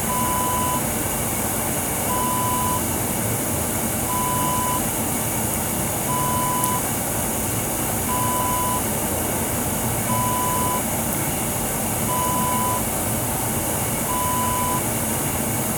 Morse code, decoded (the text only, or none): none